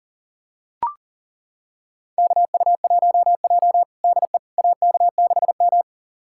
Morse code, heard as a keyed tone, about 40 wpm, 700 Hz. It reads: KU1J DE AK6M
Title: USA Callsign DE Callsign - 4 Characters - 40wpm